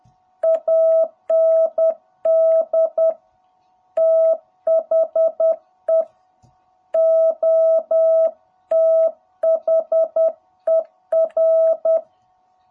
Morse code, heard as a keyed tone, about 10 wpm, 660 Hz. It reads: AND THE OTHER
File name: jh8cbh / 2021cw